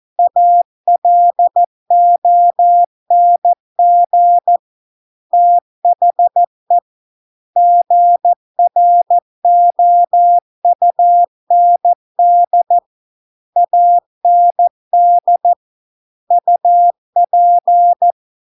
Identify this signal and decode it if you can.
Morse code, heard as a keyed tone, about 14 wpm, 700 Hz. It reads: ALONG THE GROUND AND UP